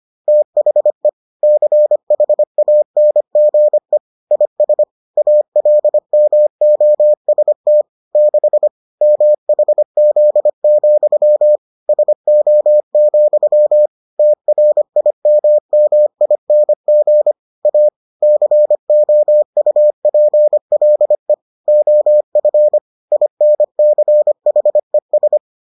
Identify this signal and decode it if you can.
Morse code, heard as a keyed tone, about 25 wpm, 600 Hz. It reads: THE CHANGE IS ALMOST 6 MHZ, SO, TRIMMING A COUPLE OF INCHES